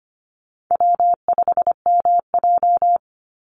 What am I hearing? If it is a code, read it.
Morse code, heard as a keyed tone, about 25 wpm, 700 Hz. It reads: W5MJ
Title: Easy Call Signs - 25wpm